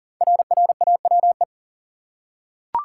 Morse code, heard as a keyed tone, about 40 wpm, 700 Hz. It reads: RRAWE
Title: Code Groups - 5 Letters - 40wpm